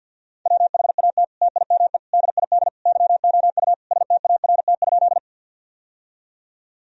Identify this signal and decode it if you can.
Morse code, heard as a keyed tone, about 46 wpm, 700 Hz. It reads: WHAT TIME DID YOU START?
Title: Sentences from Top 1000 Words - Review Part 1 - 50wpm